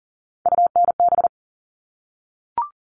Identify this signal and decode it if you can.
Morse code, heard as a keyed tone, about 40 wpm, 700 Hz. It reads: UNB
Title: All Cut Numbers - 3-Digit 40wpm